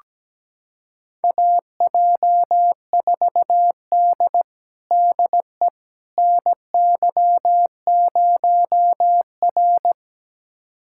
Morse code, heard as a keyed tone, about 17 wpm, 700 Hz. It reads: AJ4D DE NY0R
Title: USA Callsign DE Callsign - 4 Characters - 17wpm